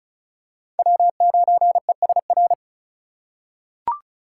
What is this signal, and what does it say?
Morse code, heard as a keyed tone, about 36 wpm, 700 Hz. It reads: W9ESR
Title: USA Call Signs - 6x Character Spacing - 35wpm